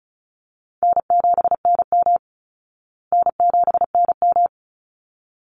Morse code, heard as a keyed tone, about 34 wpm, 700 Hz. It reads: N7NM N7NM
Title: Easy Call Signs Repeated 2x - 35wpm